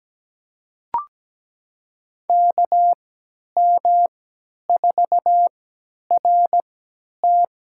Morse code, heard as a keyed tone, about 17 wpm, spaced as for 10 wpm, 700 Hz. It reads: KM4RT
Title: USA Call Signs - 3x Character Spacing - 17wpm